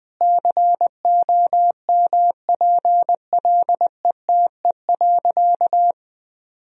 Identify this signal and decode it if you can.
Morse code, heard as a keyed tone, about 20 wpm, 700 Hz. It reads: COMPLETE.
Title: Sentences from Top 1000 Words - Review Part 2 - 20wpm